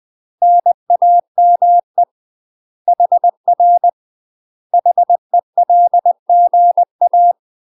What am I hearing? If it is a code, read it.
Morse code, heard as a keyed tone, about 20 wpm, 700 Hz. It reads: NAME HR HELGA
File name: UK MASTER QSOs TRAINING / 20WPM / UK MASTER QSOs 0068